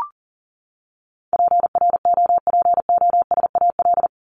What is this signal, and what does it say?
Morse code, heard as a keyed tone, about 40 wpm, 700 Hz. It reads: PROPOSAL
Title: Top 1751-2000 Words 40wpm